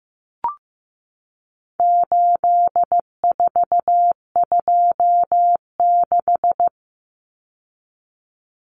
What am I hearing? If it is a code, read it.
Morse code, heard as a keyed tone, about 15 wpm, 700 Hz. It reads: 8426
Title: Numbers - 4-Digit 15wpm